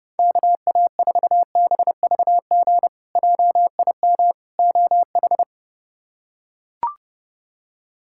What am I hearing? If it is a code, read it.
Morse code, heard as a keyed tone, about 30 wpm, 700 Hz. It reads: KA4BVG JIM OH
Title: K1USN SST QSO Practice - 30wpm